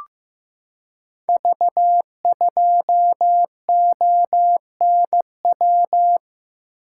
Morse code, heard as a keyed tone, about 15 wpm, 700 Hz. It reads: V2ONW
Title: Code Groups - 5 Letter:Numbers - 15wpm